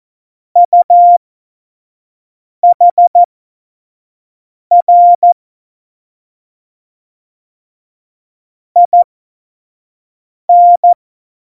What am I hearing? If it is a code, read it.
Morse code, heard as a keyed tone, about 14 wpm, spaced as for 5 wpm, 700 Hz. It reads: UHR IN
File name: Morsekurs / Witze / Witz 23. Oktober 2018, 25 F70